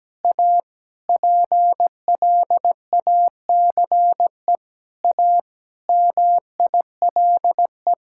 Morse code, heard as a keyed tone, about 17 wpm, 700 Hz. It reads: A PLACE A MILE
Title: Sentences from Top 1000 Words - Review Part 6 - 17wpm